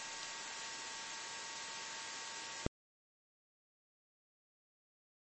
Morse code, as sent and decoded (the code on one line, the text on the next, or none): none